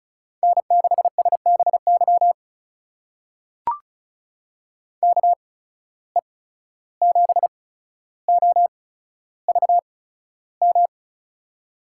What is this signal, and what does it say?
Morse code, heard as a keyed tone, about 36 wpm, spaced as for 14 wpm, 700 Hz. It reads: N6SBY KE7OVM
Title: USA Call Signs - 8x Character Spacing - 35wpm